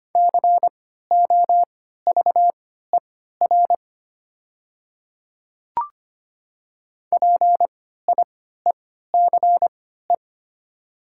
Morse code, heard as a keyed tone, about 25 wpm, spaced as for 14 wpm, 690 Hz. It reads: COVER PIECE
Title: 3x Character Spacing - Top 500 Words - 25wpm